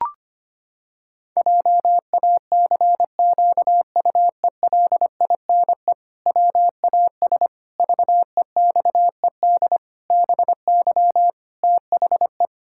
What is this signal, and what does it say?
Morse code, heard as a keyed tone, about 25 wpm, 700 Hz. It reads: JACQUELINE WAS VEXED BY THE
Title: Pangram Challenge v2 - 25wpm